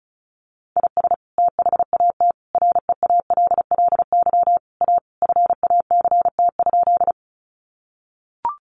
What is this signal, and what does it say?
Morse code, heard as a keyed tone, about 34 wpm, 700 Hz. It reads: IS THAT REALLY A FACT?
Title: Sentences from Top 600 Words 35wpm